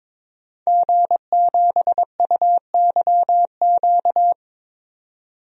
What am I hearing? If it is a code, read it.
Morse code, heard as a keyed tone, about 22 wpm, 700 Hz. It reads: G7UYQ
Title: DXCC-100 Call Signs - 5 Characters - 22wpm